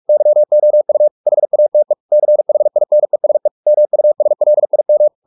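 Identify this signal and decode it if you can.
Morse code, heard as a keyed tone, about 44 wpm, 600 Hz. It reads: YOU HATE CHINESE MUSLIM
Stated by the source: Morse practice at 45 wpm, the novel Never by Ken Follett in English, from CW med Gnister.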